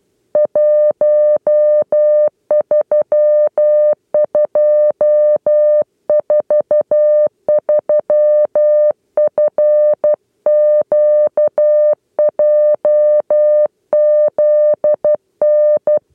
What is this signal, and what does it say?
Morse code, heard as a keyed tone, about 12 wpm, 600 Hz. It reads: 13243 FQJZN
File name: 8 lezione cw 2